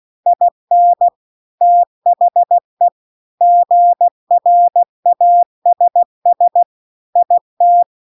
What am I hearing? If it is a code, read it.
Morse code, heard as a keyed tone, about 16 wpm, 700 Hz. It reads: IN THE GRASS IT
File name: 16WPM